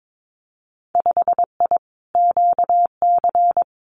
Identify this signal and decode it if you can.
Morse code, heard as a keyed tone, about 22 wpm, 700 Hz. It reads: 5I QC
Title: Winter Field Day Exchanges Encore 22wpm